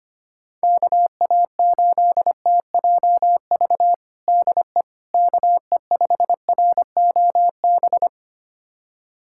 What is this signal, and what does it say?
Morse code, heard as a keyed tone, about 25 wpm, 700 Hz. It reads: KA8TJV DE KE5ROB
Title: USA Callsign DE Callsign 25wpm